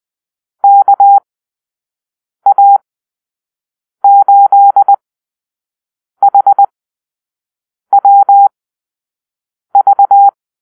Morse code, heard as a keyed tone, about 20 wpm, spaced as for 6 wpm, 800 Hz. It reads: KA8HWV
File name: arrl / Call-Sign Drill-5